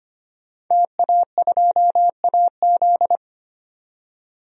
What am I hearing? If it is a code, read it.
Morse code, heard as a keyed tone, about 25 wpm, 700 Hz. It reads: TA2AZ
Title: DXCC-100 Call Signs - 5 Characters - 25wpm